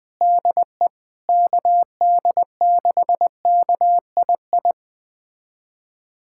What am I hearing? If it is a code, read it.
Morse code, heard as a keyed tone, about 20 wpm, 700 Hz. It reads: DE KD6KII
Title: USA Callsign DE Callsign - 6 Characters - 20wpm